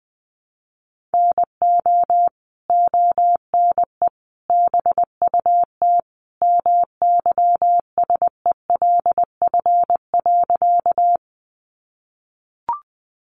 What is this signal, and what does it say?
Morse code, heard as a keyed tone, about 19 wpm, 700 Hz. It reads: NO ONE BUT MYSELF.
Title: Sentences from Top 1300 Words 20wpm